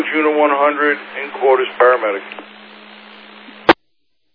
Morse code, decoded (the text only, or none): none